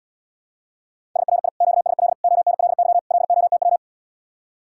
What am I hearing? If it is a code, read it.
Morse code, heard as a keyed tone, about 79 wpm, 700 Hz. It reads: SRI QSL ONLY LOTW